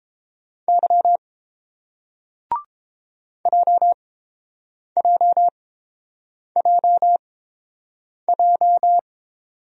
Morse code, heard as a keyed tone, about 30 wpm, spaced as for 7 wpm, 700 Hz. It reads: Y JJJJ